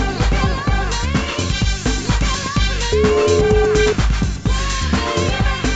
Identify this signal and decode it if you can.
Ringback tone (Europe, 425 Hz), with noise.